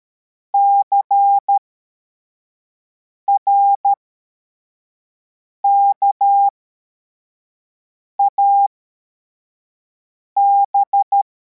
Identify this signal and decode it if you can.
Morse code, heard as a keyed tone, about 13 wpm, spaced as for 4 wpm, 800 Hz. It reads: CRKAB
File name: formation / cw / Text43